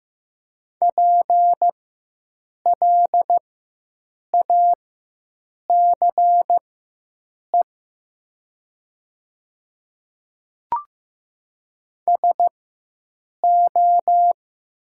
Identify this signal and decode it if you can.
Morse code, heard as a keyed tone, about 15 wpm, spaced as for 7 wpm, 700 Hz. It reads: PLACE SO
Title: 4x Character Spacing - Top 300 Words - 15wpm